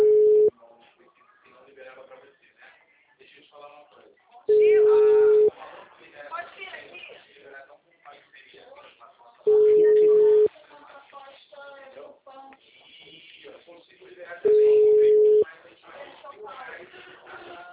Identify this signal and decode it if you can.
Ringback tone (Europe, 425 Hz), with speech.
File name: out- -2016- - - .37759